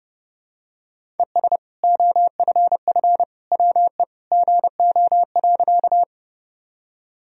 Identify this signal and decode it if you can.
Morse code, heard as a keyed tone, about 30 wpm, 700 Hz. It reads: ES OFF WE GO.
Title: Sentences from Top 200 Words and Abbreviations 30wpm